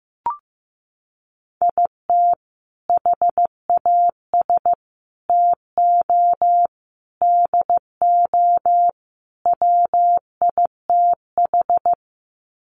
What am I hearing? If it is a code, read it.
Morse code, heard as a keyed tone, about 15 wpm, 700 Hz. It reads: IT HAS TO DO WITH
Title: Sets of 5 Words 15wpm